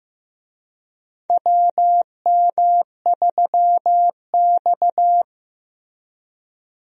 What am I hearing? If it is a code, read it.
Morse code, heard as a keyed tone, about 16 wpm, 700 Hz. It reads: WM3X
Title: USA Call Signs - 4 Characters - Repeated 3x - 15wpm